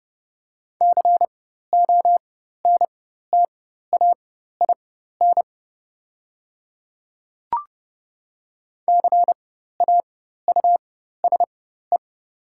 Morse code, heard as a keyed tone, about 29 wpm, spaced as for 14 wpm, 700 Hz. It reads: CONTAIN CAUSE
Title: 4x Character Spacing - Top 500 Words - 30wpm